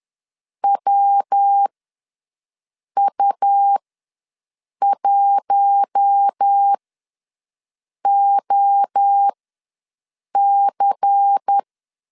Morse code, heard as a keyed tone, about 10 wpm, spaced as for 5 wpm, 800 Hz. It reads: WU1OC
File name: LEZIONI / 5 LBVC / 5 LBVC ws 10wpm 3733